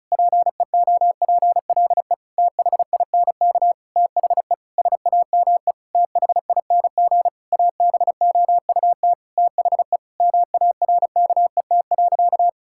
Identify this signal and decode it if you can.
Morse code, heard as a keyed tone, about 36 wpm, 700 Hz. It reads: PEOPLE THINK THE SAME THING ABOUT THE MARKET.
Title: Sentences from Top 1000 Words - Review Part 5 - 35wpm